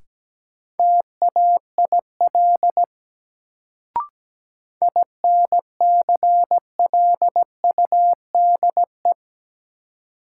Morse code, heard as a keyed tone, about 17 wpm, 700 Hz. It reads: TAIL INCLUDE
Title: Sentences from Top 1000 Words - Review Part 3 - New Word Drill - 17wpm